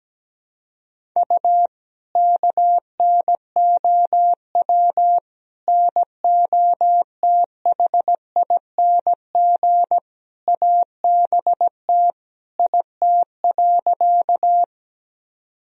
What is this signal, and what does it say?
Morse code, heard as a keyed tone, about 17 wpm, 700 Hz. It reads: U KNOW NOTHING ABT IT.
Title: Sentences from Top 500 Words and Abbreviations 17wpm